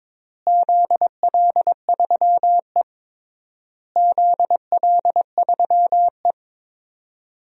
Morse code, heard as a keyed tone, about 21 wpm, 700 Hz. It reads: ZL3E ZL3E
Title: DXCC-100 Call Signs - 4 Characters - Repeated 3x - 22wpm